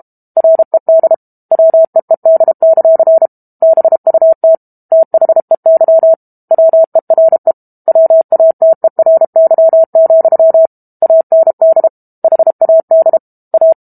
Morse code, heard as a keyed tone, about 32 wpm, 650 Hz. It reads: RED WEED; BUT THEY WERE WATERY, AND HAD A
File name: Book Two / 06-Chapter 6 - 32 WPM